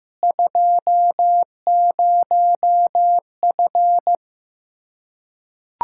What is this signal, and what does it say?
Morse code, heard as a keyed tone, about 15 wpm, 700 Hz. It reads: 20F
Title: Practice QSO - Exchange 2 - Info - 15wpm